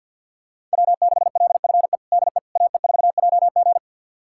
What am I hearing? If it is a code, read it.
Morse code, heard as a keyed tone, about 52 wpm, 690 Hz. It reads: W6LFE DE AE4JG
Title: USA Callsign DE Callsign - 5 Characters - 50wpm